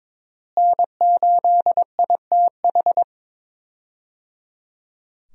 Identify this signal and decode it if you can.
Morse code, heard as a keyed tone, about 22 wpm, 700 Hz. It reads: N8ITH
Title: USA Call Signs - 5 Characters - Repeated 3x - 22wpm